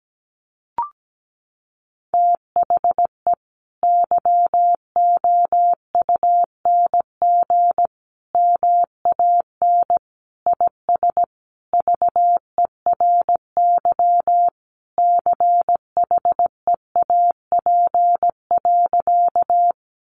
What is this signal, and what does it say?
Morse code, heard as a keyed tone, about 17 wpm, 700 Hz. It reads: THE YOUNG MAN IS VERY CHEAP.
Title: Sentences from Top 1200 Words 17wpm